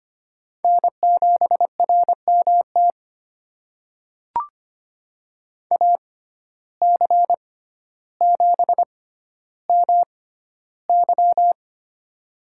Morse code, heard as a keyed tone, about 25 wpm, spaced as for 13 wpm, 690 Hz. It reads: N7RMT AC7MY
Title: USA Call Signs - 5 Characters - 6x Character Spacing - 25wpm